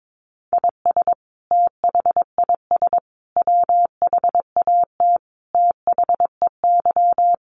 Morse code, heard as a keyed tone, about 22 wpm, 700 Hz. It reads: IS THIS WHAT THEY
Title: Sentences from Top 1500 Words 22wpm